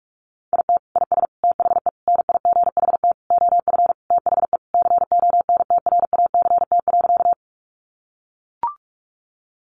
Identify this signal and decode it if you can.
Morse code, heard as a keyed tone, about 44 wpm, 700 Hz. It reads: IT IS THE NIGHT OF THE CONTRACT.
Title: Sentences from Top 1400 Words 45wpm